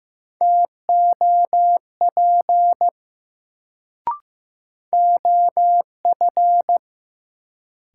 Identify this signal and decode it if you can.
Morse code, heard as a keyed tone, about 15 wpm, 700 Hz. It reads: TOP OF
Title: Sets of 2 Words - New Word Drill - 15wpm